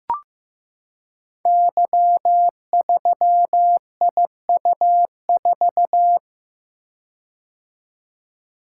Morse code, heard as a keyed tone, about 15 wpm, 700 Hz. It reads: Y3IU4
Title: Code Groups - 5 Letter:Numbers - 15wpm